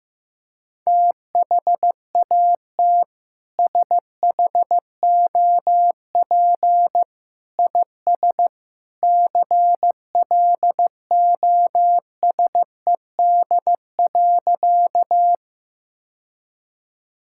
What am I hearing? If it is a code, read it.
Morse code, heard as a keyed tone, about 14 wpm, 700 Hz. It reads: THAT SHOP IS CLOSED.
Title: Sentences from Top 1600 Words 15wpm